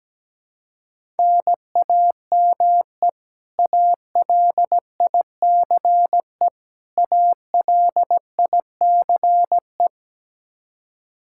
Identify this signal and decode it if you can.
Morse code, heard as a keyed tone, about 17 wpm, 700 Hz. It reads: NAME ALICE ALICE
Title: Practice QSO - Exchange 1 - Beginning - 17wpm